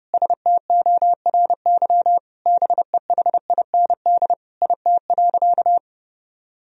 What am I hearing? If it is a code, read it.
Morse code, heard as a keyed tone, about 30 wpm, 700 Hz. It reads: STORY BEHIND IT.
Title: Sentences from Top 500 Words and Abbreviations 30wpm